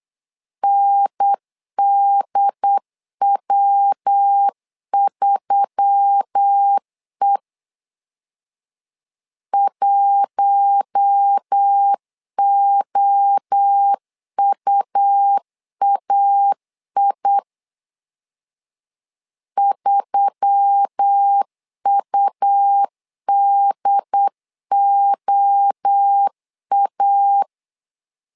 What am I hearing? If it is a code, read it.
Morse code, heard as a keyed tone, about 8 wpm, 800 Hz. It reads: NDW3E 1OUAI 3UDOA
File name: LEZIONI / 4 Q1S3 8wpm 3037